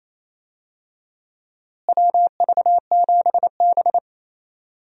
Morse code, heard as a keyed tone, about 27 wpm, 700 Hz. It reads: WV7B
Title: USA Call Signs - 4 Characters - Repeated 2x - 28wpm